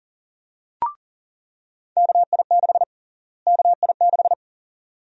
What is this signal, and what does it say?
Morse code, heard as a keyed tone, about 40 wpm, 700 Hz. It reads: KI6 KI6
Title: USA Call Sign Prefixes - Repeated 2x - 40wpm